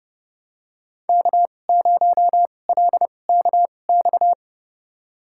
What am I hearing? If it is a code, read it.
Morse code, heard as a keyed tone, about 30 wpm, spaced as for 22 wpm, 700 Hz. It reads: K0LKX
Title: USA Call Signs - 5 Characters - 2x Character Spacing - 30wpm